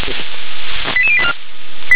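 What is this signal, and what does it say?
Five-tone selective call (ZVEI), heard as quick runs of five tones, tones of about 70 ms. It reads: 88084
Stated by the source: Five-tone ZVEI call received on an SDR.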